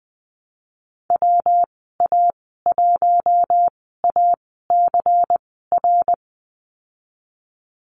Morse code, heard as a keyed tone, about 20 wpm, spaced as for 15 wpm, 700 Hz. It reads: WA1ACR
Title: 154 Lesson 25g - Letter p - Call Signs 20wpm